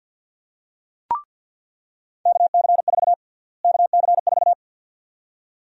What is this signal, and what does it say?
Morse code, heard as a keyed tone, about 48 wpm, 700 Hz. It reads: KC4 KC4
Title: USA Call Sign Prefixes - Repeated 2x - 50wpm